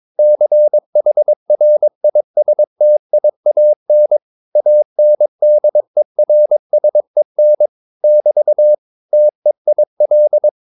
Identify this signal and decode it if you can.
Morse code, heard as a keyed tone, about 22 wpm, 600 Hz. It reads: CHRISTIAN ANDERSEN <BT> TEIL